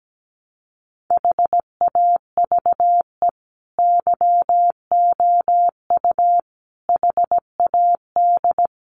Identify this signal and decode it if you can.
Morse code, heard as a keyed tone, about 17 wpm, 700 Hz. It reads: HAVE YOU HAD
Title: Sentences from Top 1200 Words 17wpm